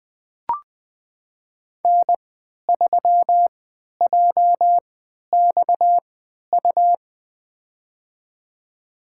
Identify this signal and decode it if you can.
Morse code, heard as a keyed tone, about 21 wpm, spaced as for 12 wpm, 700 Hz. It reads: N3JXU
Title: USA Call Signs - 3x Character Spacing - 20wpm